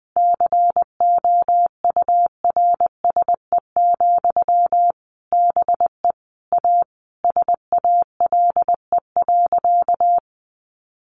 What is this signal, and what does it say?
Morse code, heard as a keyed tone, about 20 wpm, 700 Hz. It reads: COURSE, BE A SALE.
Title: Sentences from Top 1400 Words 20wpm